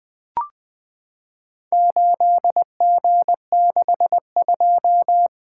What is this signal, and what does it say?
Morse code, heard as a keyed tone, about 20 wpm, 700 Hz. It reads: 8G62
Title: Four Letter-Number Combinations - 20wpm